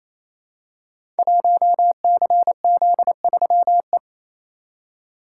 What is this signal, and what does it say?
Morse code, heard as a keyed tone, about 29 wpm, 700 Hz. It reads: 1CZ3E
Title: Code Groups - 5 Letter:Numbers - 28wpm